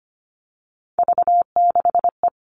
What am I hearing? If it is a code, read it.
Morse code, heard as a keyed tone, about 25 wpm, 700 Hz. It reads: V6E
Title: All Cut Numbers - 3-Digit 25wpm